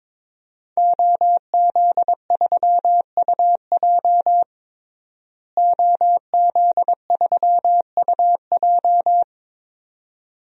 Morse code, heard as a keyed tone, about 22 wpm, 700 Hz. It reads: OZ3UJ OZ3UJ
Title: DXCC-100 Call Signs - 5 Characters - Repeated 2x - 22wpm